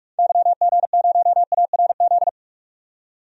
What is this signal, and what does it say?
Morse code, heard as a keyed tone, about 46 wpm, 700 Hz. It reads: YG0ARZ
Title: DXCC-100 Call Signs - 6 Characters - 45wpm